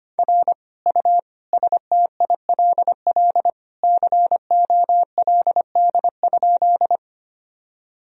Morse code, heard as a keyed tone, about 25 wpm, 700 Hz. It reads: R U STILL COLD?
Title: Sentences from Top 400 Words and Abbreviations 25wpm